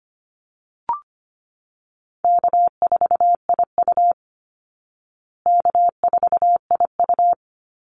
Morse code, heard as a keyed tone, about 25 wpm, 700 Hz. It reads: K4IU K4IU
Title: Easy Call Signs Repeated 3x - 25wpm